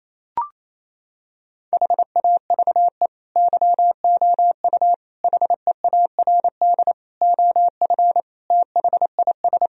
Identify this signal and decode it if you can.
Morse code, heard as a keyed tone, about 28 wpm, 700 Hz. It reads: HAVE YOU HEARD OF THIS
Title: Sentences from Top 1000 Words - Review Part 3 - 28wpm